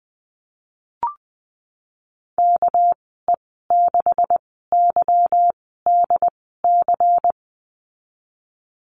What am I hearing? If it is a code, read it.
Morse code, heard as a keyed tone, about 20 wpm, spaced as for 15 wpm, 700 Hz. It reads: KE6YDC
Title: 231 Lesson 37g - Letter x - Call Signs 20wpm